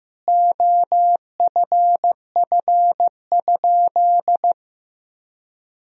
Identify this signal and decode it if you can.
Morse code, heard as a keyed tone, about 15 wpm, 700 Hz. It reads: OFF?
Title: Sentences from Top 500 Words and Abbreviations 15wpm